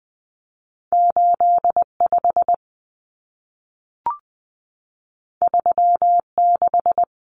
Morse code, heard as a keyed tone, about 20 wpm, 700 Hz. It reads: 85 36
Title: Common Cut Numbers - 2-Digit 20wpm